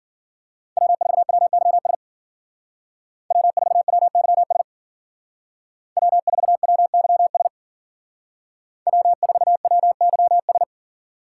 Morse code, heard as a keyed tone, about 52 wpm, 700 Hz. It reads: W4WYS W4WYS W4WYS W4WYS